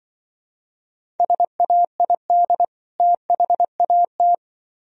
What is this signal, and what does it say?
Morse code, heard as a keyed tone, about 25 wpm, 700 Hz. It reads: SAID THAT
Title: Speed-Racing - Sets of 2 Words 20wpm